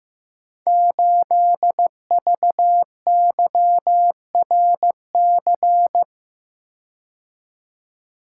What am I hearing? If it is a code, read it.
Morse code, heard as a keyed tone, about 15 wpm, 700 Hz. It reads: 8VYRC
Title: Code Groups - 5 Letter:Numbers - 15wpm